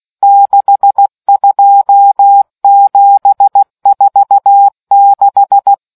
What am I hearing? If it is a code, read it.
Morse code, heard as a keyed tone, about 16 wpm, 800 Hz. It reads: 62746